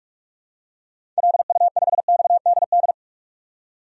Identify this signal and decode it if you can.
Morse code, heard as a keyed tone, about 46 wpm, 700 Hz. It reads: LU5XDD
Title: DXCC-100 Call Signs - 6 Characters - 45wpm